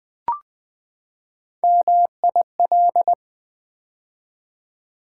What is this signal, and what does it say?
Morse code, heard as a keyed tone, about 20 wpm, 700 Hz. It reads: MIL
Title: Build A Word - 3 Letters At A Time - 8-Letter Words - 20wpm